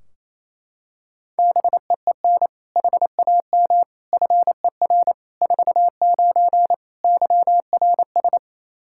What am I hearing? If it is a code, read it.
Morse code, heard as a keyed tone, about 28 wpm, 700 Hz. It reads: BEEN HAM FER 49 YRS
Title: Practice QSO - Exchange 3 - More Info - 28wpm